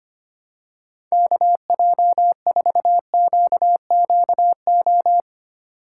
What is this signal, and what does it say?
Morse code, heard as a keyed tone, about 25 wpm, 700 Hz. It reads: KJ4QQO
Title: USA Call Signs - 6 Characters - 3x Character Spacing - 25wpm